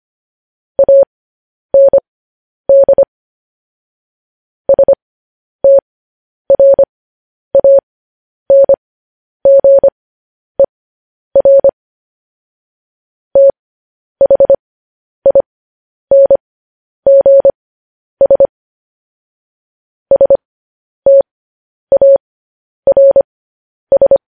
Morse code, heard as a keyed tone, about 25 wpm, spaced as for 10 wpm, 550 Hz. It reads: AND STRANGER THINGS STARS
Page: news-headlines-in-morse-code-at-10-wpm